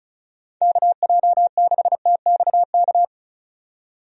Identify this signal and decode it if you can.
Morse code, heard as a keyed tone, about 34 wpm, 700 Hz. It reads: KJ6TXK
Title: USA Call Signs - 6 Characters - 35wpm